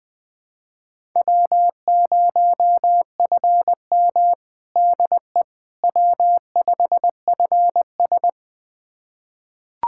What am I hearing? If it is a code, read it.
Morse code, heard as a keyed tone, about 20 wpm, 700 Hz. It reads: W0FM DE W5FS
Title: USA Callsign DE Callsign - 4 Characters - 20wpm